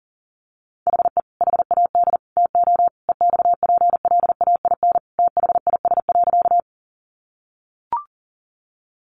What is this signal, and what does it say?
Morse code, heard as a keyed tone, about 40 wpm, 700 Hz. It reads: HE HAD TO EXPLAIN THIS.
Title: Sentences from Top 1100 Words 40wpm